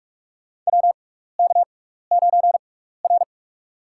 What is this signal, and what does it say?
Morse code, heard as a keyed tone, about 42 wpm, spaced as for 15 wpm, 700 Hz. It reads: WK9R
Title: USA Call Signs - 6x Character Spacing - 45wpm